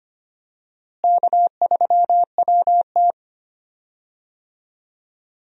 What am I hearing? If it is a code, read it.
Morse code, heard as a keyed tone, about 25 wpm, 700 Hz. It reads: K3WT
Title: USA Call Signs - 4 Characters - Repeated 3x - 25wpm